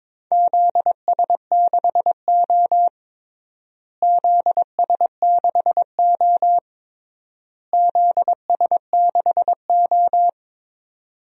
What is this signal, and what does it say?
Morse code, heard as a keyed tone, about 22 wpm, 700 Hz. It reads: ZS6O ZS6O ZS6O
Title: DXCC-100 Call Signs - 4 Characters - Repeated 3x - 22wpm